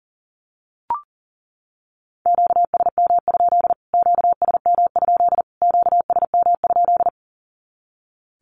Morse code, heard as a keyed tone, about 40 wpm, 700 Hz. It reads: QSM? QSM? QSM?
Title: Q-Codes Repeated 3x 40wpm